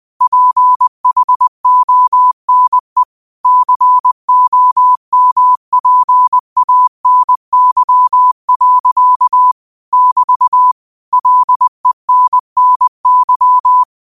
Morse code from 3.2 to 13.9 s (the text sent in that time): COMPANY. <BT> LENNY